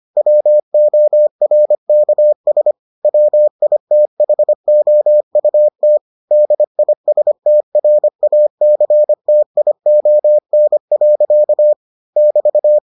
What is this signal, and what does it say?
Morse code, heard as a keyed tone, about 25 wpm, 600 Hz. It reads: WORKS WITHOUT DISTRACTION. <BT>